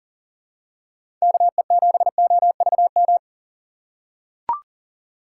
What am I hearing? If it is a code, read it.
Morse code, heard as a keyed tone, about 39 wpm, 700 Hz. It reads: KE7OVM
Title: USA Call Signs - 6 Characters - 6x Character Spacing - 40wpm